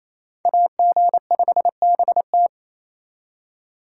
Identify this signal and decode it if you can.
Morse code, heard as a keyed tone, about 35 wpm, 700 Hz. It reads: AG5BT
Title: Speed Racing - USA Call Signs 28wpm